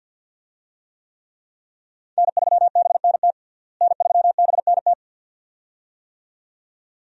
Morse code, heard as a keyed tone, about 52 wpm, 700 Hz. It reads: N3BNT N3BNT
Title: USA Call Signs - 5 Characters - Repeated 2x - 50wpm